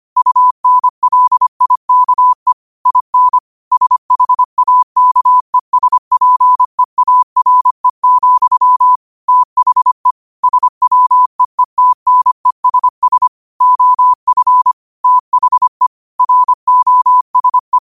Morse code, heard as a keyed tone, about 25 wpm, 1000 Hz. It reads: UNLIKE IN SHAKESPEARE, THE SWEETNESS OF THE ROSE
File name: qotdfiles / 25wpm